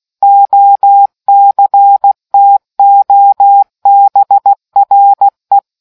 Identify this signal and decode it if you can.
Morse code, heard as a keyed tone, about 16 wpm, 800 Hz. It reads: OCTOBRE